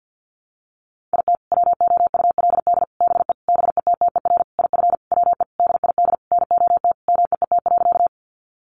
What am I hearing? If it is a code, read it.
Morse code, heard as a keyed tone, about 50 wpm, 700 Hz. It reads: IT WOULD BE BETTER IF WE DID NOT MEET.
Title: Sentences from Top 1100 Words 50wpm